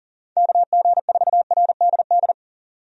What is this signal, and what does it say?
Morse code, heard as a keyed tone, about 40 wpm, 700 Hz. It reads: KG4RDD
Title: USA Call Signs - 6 Characters - 40wpm